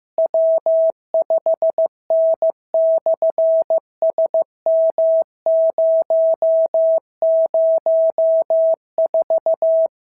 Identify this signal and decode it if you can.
Morse code, heard as a keyed tone, about 15 wpm, 650 Hz. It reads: W5N/SM004
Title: SOTA QSO Practice 15wpm